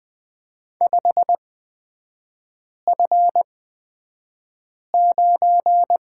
Morse code, heard as a keyed tone, about 20 wpm, spaced as for 5 wpm, 700 Hz. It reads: 5F9
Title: 224 Lesson 36e - BT - Three Characters 20wpm